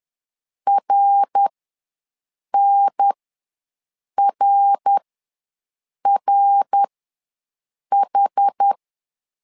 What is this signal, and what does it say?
Morse code, heard as a keyed tone, about 11 wpm, spaced as for 6 wpm, 800 Hz. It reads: RNRRH
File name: LEZIONI / 3 MWRO / 3 MWR ws 10wpm 2151